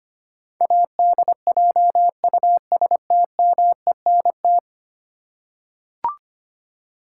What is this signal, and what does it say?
Morse code, heard as a keyed tone, about 25 wpm, 700 Hz. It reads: ADJUSTMENT
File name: Ten Letter Words - 25wpm